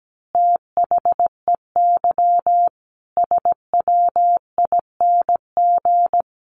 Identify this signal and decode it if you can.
Morse code, heard as a keyed tone, about 17 wpm, 700 Hz. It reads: THEY SWING
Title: Easy Phrases 17wpm